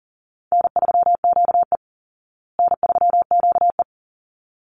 Morse code, heard as a keyed tone, about 40 wpm, 700 Hz. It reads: N3QE N3QE
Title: Easy Call Signs Repeated 2x - 40wpm